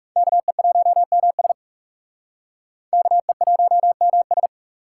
Morse code, heard as a keyed tone, about 39 wpm, 700 Hz. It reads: KE1MS KE1MS